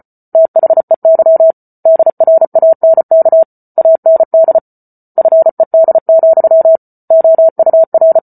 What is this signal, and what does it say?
Morse code, heard as a keyed tone, about 34 wpm, 650 Hz. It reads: THEY DRANK AND FED, OUR
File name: Book Two / 08-Chapter 8 - 34 WPM